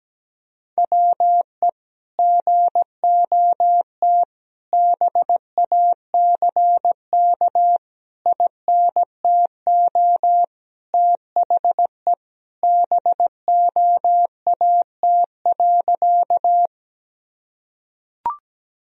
Morse code, heard as a keyed tone, about 17 wpm, 700 Hz. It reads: WE GOT BACK INTO THE BOAT.
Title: Sentences from Top 1000 Words - Review Part 3 - 17wpm